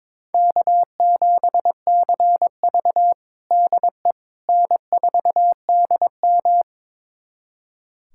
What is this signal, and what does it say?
Morse code, heard as a keyed tone, about 22 wpm, 700 Hz. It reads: K7CV DE N4DM
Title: USA Callsign DE Callsign - 4 Characters - 22wpm